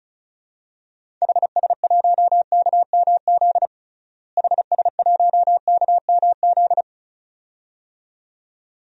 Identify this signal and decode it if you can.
Morse code, heard as a keyed tone, about 36 wpm, 700 Hz. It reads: HS1KMZ HS1KMZ
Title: DXCC-100 Call Signs - 6 Characters - Repeated 2x - 35wpm